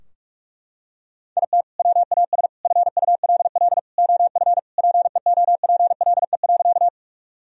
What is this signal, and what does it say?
Morse code, heard as a keyed tone, about 41 wpm, 700 Hz. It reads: IT WAS FULL OF PEOPLE.
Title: Sentences from Top 1000 Words - Review Part 3 - 45wpm